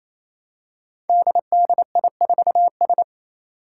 Morse code, heard as a keyed tone, about 29 wpm, 700 Hz. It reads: DDI4S
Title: Five Challenging Letter-Number Combinations 28wpm